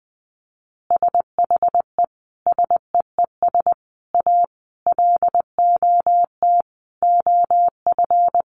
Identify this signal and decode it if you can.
Morse code, heard as a keyed tone, about 20 wpm, 700 Hz. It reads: SHE SEES A LOT OF
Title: Easy Phrases 20wpm